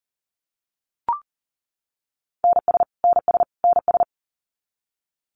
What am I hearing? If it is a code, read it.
Morse code, heard as a keyed tone, about 40 wpm, 700 Hz. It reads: NS NS NS
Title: Canadian Provinces and Territories - Repeated 3x - 40wpm